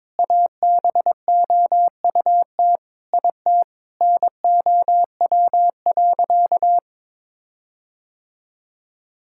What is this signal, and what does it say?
Morse code, heard as a keyed tone, about 22 wpm, 700 Hz. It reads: ABOUT IT NOW.
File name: Sentences with 7-Word Phrases - 22wpm